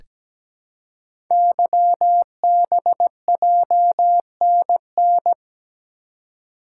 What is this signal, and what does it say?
Morse code, heard as a keyed tone, about 17 wpm, 700 Hz. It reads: YBJNN
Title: Code Groups - 5 Letters - 17wpm